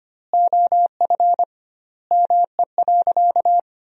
Morse code, heard as a keyed tone, about 25 wpm, 700 Hz. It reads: OF ME.
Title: 2x Word-Spacing - Sentences from Top 300 Words - 25wpm